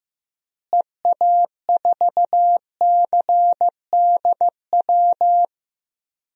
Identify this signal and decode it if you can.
Morse code, heard as a keyed tone, about 16 wpm, 700 Hz. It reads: EA4CDW
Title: DXCC-100 Call Signs - 6 Characters - Repeated 3x - 15wpm